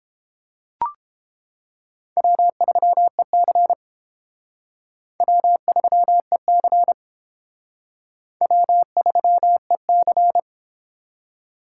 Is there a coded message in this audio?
Yes: Morse code from 2.2 to 10.4 s, heard as a keyed tone, about 31 wpm, 700 Hz. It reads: W3EC W3EC W3EC